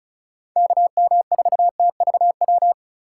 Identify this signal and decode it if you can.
Morse code, heard as a keyed tone, about 43 wpm, 700 Hz. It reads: KM4TVW
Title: Speed Racing - USA Call Signs 35wpm